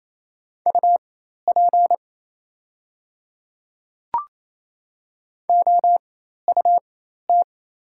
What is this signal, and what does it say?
Morse code, heard as a keyed tone, about 27 wpm, spaced as for 13 wpm, 700 Hz. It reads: UP OUT
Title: 4x Character Spacing - Top 200 Words - 28wpm